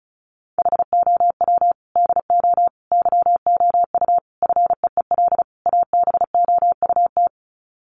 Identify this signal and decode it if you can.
Morse code, heard as a keyed tone, about 34 wpm, 700 Hz. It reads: HOW DO YOU FEEL ABOUT
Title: Sets of 5 Words 35wpm